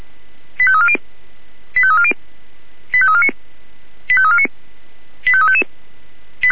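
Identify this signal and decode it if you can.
Five-tone selective call (ZVEI), heard as quick runs of five tones, tones of about 70 ms. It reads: 85340 85340 85349 85349 85355
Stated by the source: ZVEI five-tone selective call received by SDR.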